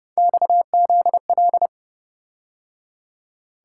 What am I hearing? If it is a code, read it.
Morse code, heard as a keyed tone, about 28 wpm, 700 Hz. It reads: XZL